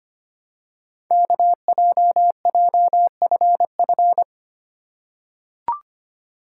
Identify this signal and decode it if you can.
Morse code, heard as a keyed tone, about 25 wpm, 700 Hz. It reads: KJJFF
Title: Code Groups - 5 Letters - 25wpm